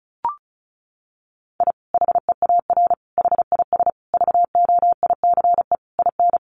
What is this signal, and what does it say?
Morse code, heard as a keyed tone, about 34 wpm, 700 Hz. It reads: I HEAR HIS VOICE IN
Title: Sentences from Top 500 Words 35wpm